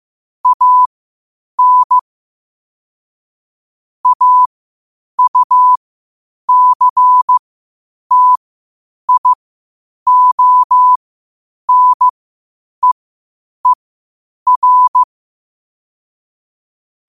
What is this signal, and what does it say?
Morse code, heard as a keyed tone, about 15 wpm, spaced as for 8 wpm, 1000 Hz. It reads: AN AUCTIONEER
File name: qotdfiles / 8wpm